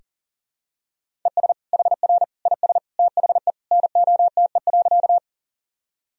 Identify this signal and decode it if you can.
Morse code, heard as a keyed tone, about 40 wpm, 700 Hz. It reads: ES HR IS THE NOTE.
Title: Sentences from Top 500 Words and Abbreviations 40wpm